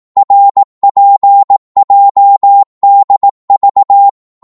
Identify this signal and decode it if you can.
Morse code, heard as a keyed tone, about 18 wpm, 800 Hz. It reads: RPJDV
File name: Morse / ltr-18wpm